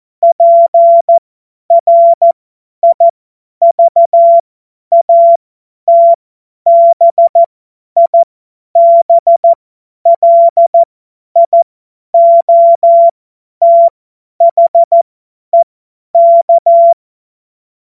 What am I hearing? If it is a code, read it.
Morse code, heard as a keyed tone, about 14 wpm, spaced as for 10 wpm, 680 Hz. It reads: PRIVATBIBLIOTHEK
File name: Morsekurs / Witze / Witz 09. November 2021, 50 F70